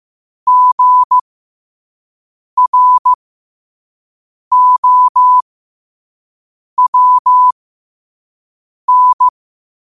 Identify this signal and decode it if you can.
Morse code, heard as a keyed tone, about 15 wpm, spaced as for 5 wpm, 1000 Hz. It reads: GROWN
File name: qotdfiles / 5wpm